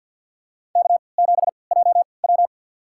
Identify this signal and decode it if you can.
Morse code, heard as a keyed tone, about 53 wpm, spaced as for 29 wpm, 700 Hz. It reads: K7JW
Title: USA Call Signs - 4 Characters - 3x Character Spacing - 50wpm